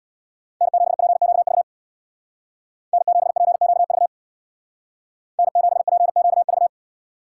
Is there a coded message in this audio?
Yes: Morse code from 0.6 to 6.7 s, heard as a keyed tone, about 64 wpm, 700 Hz. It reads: N7PQV N7PQV N7PQV